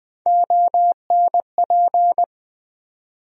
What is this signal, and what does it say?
Morse code, heard as a keyed tone, about 21 wpm, 700 Hz. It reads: ONP
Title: USA Call Sign Postfixes - 3 Letters 20wpm